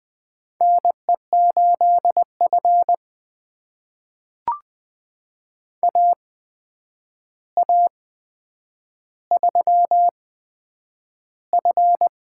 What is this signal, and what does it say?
Morse code, heard as a keyed tone, about 20 wpm, spaced as for 9 wpm, 700 Hz. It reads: NE8F AA3F
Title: USA Call Signs - 8x Character Spacing - 20wpm